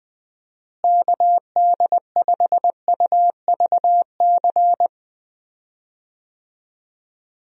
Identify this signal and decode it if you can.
Morse code, heard as a keyed tone, about 19 wpm, 700 Hz. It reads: KD5UVC
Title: USA Call Signs - 6 Characters - Repeated 3x - 20wpm